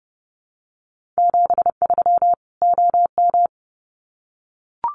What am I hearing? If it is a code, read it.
Morse code, heard as a keyed tone, about 30 wpm, 700 Hz. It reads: 73 OM
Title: Common QSO Phrases 30wpm